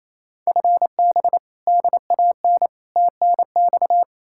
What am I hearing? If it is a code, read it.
Morse code, heard as a keyed tone, about 37 wpm, 700 Hz. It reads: FB DAN TNX